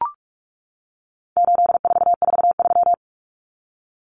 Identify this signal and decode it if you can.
Morse code, heard as a keyed tone, about 44 wpm, 700 Hz. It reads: 8443
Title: Common Cut Numbers - 4-Digit 45wpm